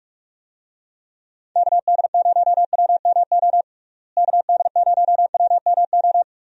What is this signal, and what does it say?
Morse code, heard as a keyed tone, about 46 wpm, 700 Hz. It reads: KD0WMO KD0WMO
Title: USA Call Signs - 6 Characters - Repeated 3x - 45wpm